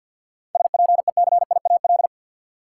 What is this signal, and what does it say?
Morse code, heard as a keyed tone, about 52 wpm, 700 Hz. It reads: SPECIAL
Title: Sentences from Top 1000 Words - Review Part 3 - New Word Drill - 50wpm